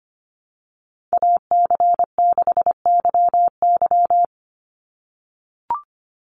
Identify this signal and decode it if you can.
Morse code, heard as a keyed tone, about 25 wpm, 700 Hz. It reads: AC6YY
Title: Call Signs - 25wpm